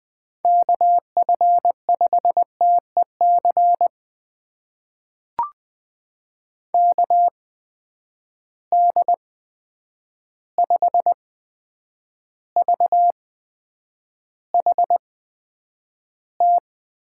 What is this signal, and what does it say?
Morse code, heard as a keyed tone, about 20 wpm, spaced as for 9 wpm, 700 Hz. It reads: KF5TEC KD5VHT
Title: USA Call Signs - 6 Characters - 8x Character Spacing - 20wpm